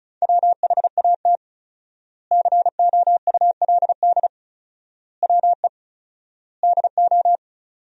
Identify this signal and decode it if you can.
Morse code, heard as a keyed tone, about 34 wpm, 700 Hz. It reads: WHAT COULD WE DO
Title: 4x Word-Spacing - Sentences from Top 400 Words - 35wpm